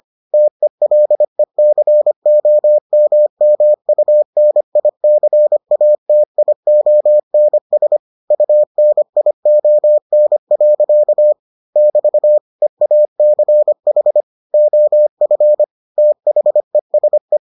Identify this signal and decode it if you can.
Morse code, heard as a keyed tone, about 25 wpm, 600 Hz. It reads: TELECOMMUNICATIONS UNION. <BT> EACH OF THESE